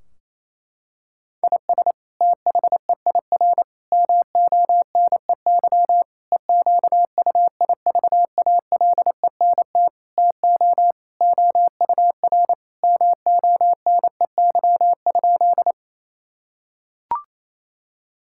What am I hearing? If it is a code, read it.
Morse code, heard as a keyed tone, about 28 wpm, 700 Hz. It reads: IS THEIR MONEY EQUIVALENT TO OUR MONEY?
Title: Sentences from Top 1600 Words 28wpm